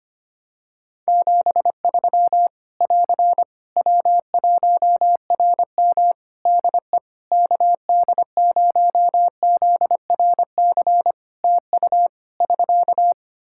Morse code, heard as a keyed tone, about 25 wpm, 700 Hz. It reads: 73 <AR> W1RM DE KD0ZRC TU <SK>
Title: Practice QSO - Exchange 4 - Ending - 25wpm